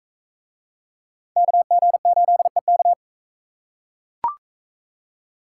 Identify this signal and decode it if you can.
Morse code, heard as a keyed tone, about 43 wpm, 700 Hz. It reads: KG8EK